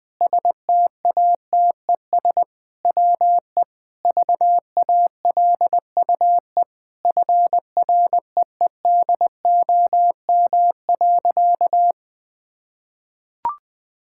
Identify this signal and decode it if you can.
Morse code, heard as a keyed tone, about 20 wpm, 700 Hz. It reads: STATES WE VALUE FREEDOM.
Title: Sentences with 5-Word Phrases - 20wpm